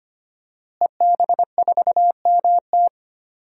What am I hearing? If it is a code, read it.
Morse code, heard as a keyed tone, about 25 wpm, 700 Hz. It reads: EB4MT
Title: DXCC-100 Call Signs - 5 Characters - Repeated 3x - 25wpm